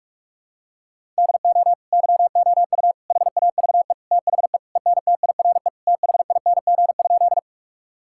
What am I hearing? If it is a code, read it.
Morse code, heard as a keyed tone, about 46 wpm, 700 Hz. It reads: DO YOU HAVE THE ENTIRE THING?
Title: Sentences from Top 1000 Words - Review Part 5 - 45wpm